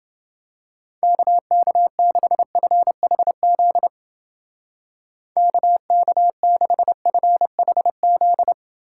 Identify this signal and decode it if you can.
Morse code, heard as a keyed tone, about 28 wpm, 700 Hz. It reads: KK6FHZ KK6FHZ